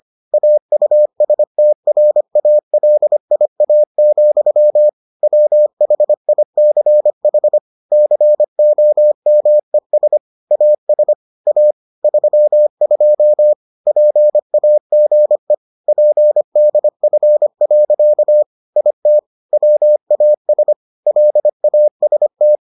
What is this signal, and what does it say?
Morse code, heard as a keyed tone, about 25 wpm, 600 Hz. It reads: AUSTRALIA, WHICH COMES AS A 32 PAGE PDF. IT WAS LAST